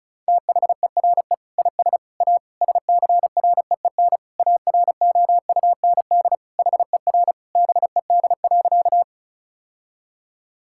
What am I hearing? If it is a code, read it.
Morse code, heard as a keyed tone, about 36 wpm, 700 Hz. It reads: THERE IS A SCREEN AROUND HER BED.
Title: Sentences from Top 1000 Words - Review Part 6 - 35wpm